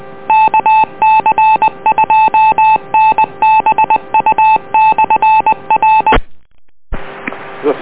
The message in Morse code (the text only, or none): KC2NBU/R